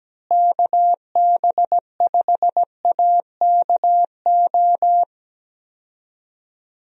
Morse code, heard as a keyed tone, about 17 wpm, 700 Hz. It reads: KB5AKO
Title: USA Call Signs - 6 Characters - Repeated 2x - 17wpm